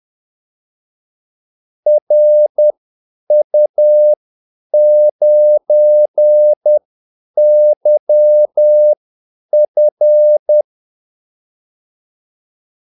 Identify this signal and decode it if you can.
Morse code, heard as a keyed tone, about 10 wpm, spaced as for 8 wpm, 600 Hz. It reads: RU9YF
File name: Callsigns-10-8wpm